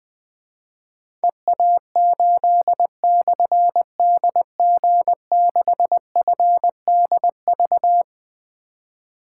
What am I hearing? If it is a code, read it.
Morse code, heard as a keyed tone, about 20 wpm, 700 Hz. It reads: EA8/DG6FDV
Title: Difficult Call Signs 20wpm